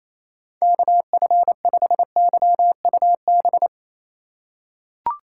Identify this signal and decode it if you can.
Morse code, heard as a keyed tone, about 28 wpm, 700 Hz. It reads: KF5YUB
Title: K1USN SST QSO Practice - 28wpm